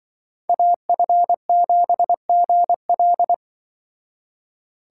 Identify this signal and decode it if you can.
Morse code, heard as a keyed tone, about 23 wpm, 700 Hz. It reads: AF7GL
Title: Speed Racing - USA 2x2 Call Signs 20wpm